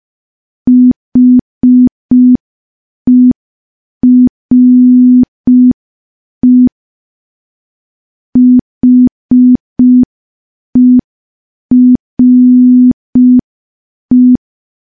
Morse code, heard as a keyed tone, about 5 wpm, 250 Hz. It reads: HERE HERE
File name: morse / morse-training-05WPM